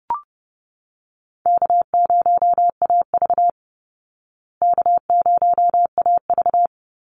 Morse code, heard as a keyed tone, about 30 wpm, 700 Hz. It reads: K0AV K0AV
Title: Easy Call Signs Repeated 3x - 30wpm